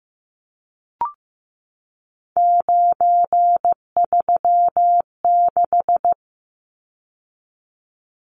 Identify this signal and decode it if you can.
Morse code, heard as a keyed tone, about 15 wpm, 700 Hz. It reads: 936
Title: Numbers - 3-Digit 15wpm